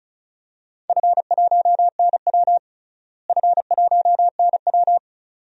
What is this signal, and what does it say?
Morse code, heard as a keyed tone, about 36 wpm, 700 Hz. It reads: F1NW F1NW
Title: DXCC-100 Call Signs - 4 Characters - Repeated 2x - 35wpm